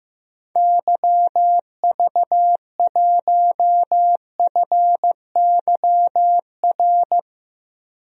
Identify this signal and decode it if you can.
Morse code, heard as a keyed tone, about 15 wpm, 700 Hz. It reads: YV1FYR
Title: DXCC-100 Call Signs - 6 Characters - 15wpm